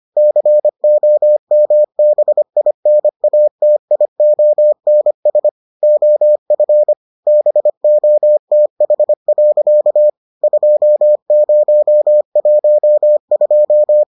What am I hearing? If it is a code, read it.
Morse code, heard as a keyed tone, about 25 wpm, 600 Hz. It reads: COMBINATIONS OF BOTH. 2012